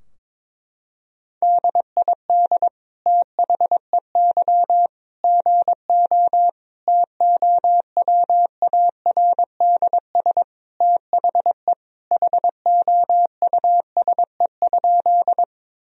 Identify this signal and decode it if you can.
Morse code, heard as a keyed tone, about 22 wpm, 700 Hz. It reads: DID THEY GO TOWARDS THE HOUSE?
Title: Sentences from Top 1000 Words - Review Part 2 - 22wpm